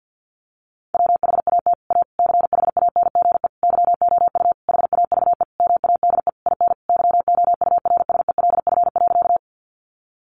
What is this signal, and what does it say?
Morse code, heard as a keyed tone, about 48 wpm, 700 Hz. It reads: WHAT A CHANGE YOU HAVE MADE IN YOURSELF.
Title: Sentences from Top 600 Words 50wpm